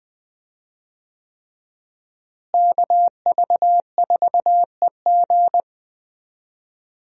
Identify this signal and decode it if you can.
Morse code, heard as a keyed tone, about 21 wpm, 700 Hz. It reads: KV4EG
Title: USA Call Signs - 5 Characters - Repeated 3x - 20wpm